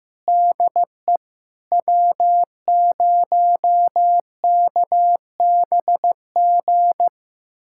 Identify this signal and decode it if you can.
Morse code, heard as a keyed tone, about 16 wpm, 700 Hz. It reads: DE W0KBG
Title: USA Callsign DE Callsign - 5 Characters - 15wpm